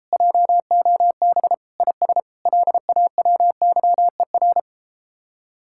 Morse code, heard as a keyed tone, about 33 wpm, 700 Hz. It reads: JOB IS LAWYER